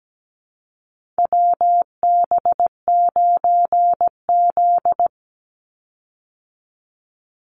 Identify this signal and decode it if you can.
Morse code, heard as a keyed tone, about 17 wpm, 700 Hz. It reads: WB9Z
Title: Call Signs Repeated 2x - 17wpm